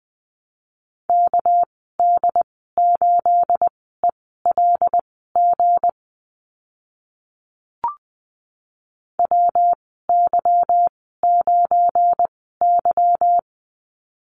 Morse code, heard as a keyed tone, about 20 wpm, spaced as for 14 wpm, 700 Hz. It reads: KD8ELG WY9Y
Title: 213 Lesson 34g - Number 8 - Call Signs 20wpm